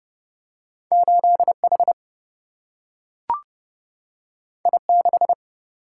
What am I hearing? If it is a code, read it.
Morse code, heard as a keyed tone, about 30 wpm, 700 Hz. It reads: 8H I6
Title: Single-Letter Single-Number 30wpm